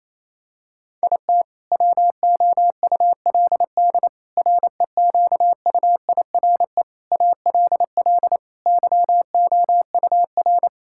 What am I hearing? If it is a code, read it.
Morse code, heard as a keyed tone, about 28 wpm, 700 Hz. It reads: IT WOULD REQUIRE ALL YOUR
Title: Sentences from Top 1000 Words - Review Part 6 - 28wpm